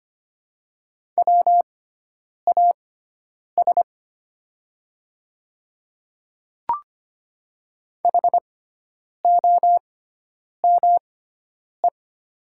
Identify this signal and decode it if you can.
Morse code, heard as a keyed tone, about 25 wpm, spaced as for 9 wpm, 700 Hz. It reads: WAS HOME
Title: 6x Character Spacing - Top 300 Words - 25wpm